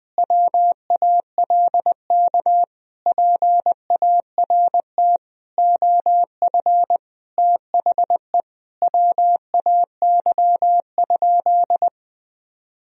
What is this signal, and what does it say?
Morse code, heard as a keyed tone, about 20 wpm, 700 Hz. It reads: WALK PART OF THE WAY?
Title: Sentences from Top 1000 Words - Review Part 1 - 20wpm